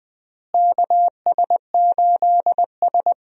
Morse code, heard as a keyed tone, about 20 wpm, 700 Hz. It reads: KS8S
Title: USA Call Signs - 4 Characters - Repeated 3x - 20wpm